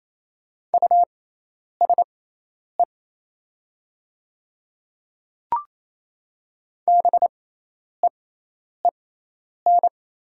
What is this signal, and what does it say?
Morse code, heard as a keyed tone, about 28 wpm, spaced as for 10 wpm, 700 Hz. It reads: USE BEEN
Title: 6x Character Spacing - Top 200 Words - 28wpm